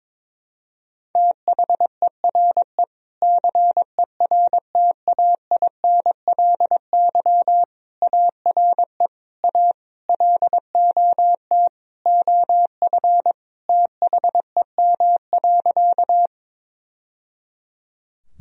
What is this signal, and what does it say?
Morse code, heard as a keyed tone, about 22 wpm, 700 Hz. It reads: THERE CERTAINLY ARE A LOT OF THEM.
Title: Sentences from Top 1000 Words - Review Part 3 - 22wpm